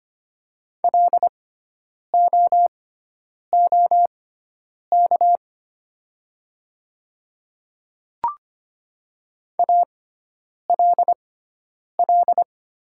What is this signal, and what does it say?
Morse code, heard as a keyed tone, about 25 wpm, spaced as for 9 wpm, 700 Hz. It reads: LOOK ALL
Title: 6x Character Spacing - Top 100 Words - 25wpm